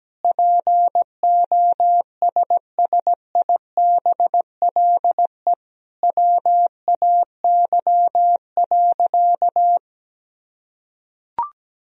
Morse code, heard as a keyed tone, about 17 wpm, 700 Hz. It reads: POSSIBLE WAY.
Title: Sentences from Top 500 Words and Abbreviations 17wpm